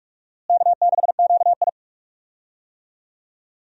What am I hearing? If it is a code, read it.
Morse code, heard as a keyed tone, about 46 wpm, 700 Hz. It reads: K6QI
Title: USA Call Signs - 4 Characters - Repeated 3x - 45wpm